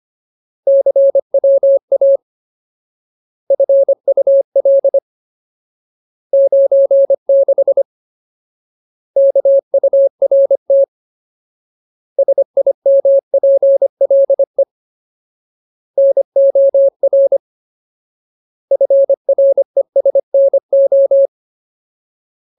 Morse code, heard as a keyed tone, about 25 wpm, 550 Hz. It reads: CWA FUL 96 KURT SIMPLE NOR FRESNO
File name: Ultimate Word List / 25WPM Ultimate